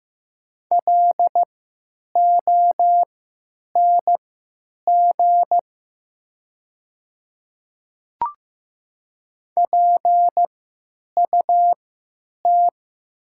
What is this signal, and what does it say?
Morse code, heard as a keyed tone, about 15 wpm, spaced as for 9 wpm, 700 Hz. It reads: LONG PUT
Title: 3x Character Spacing - Top 300 Words - 15wpm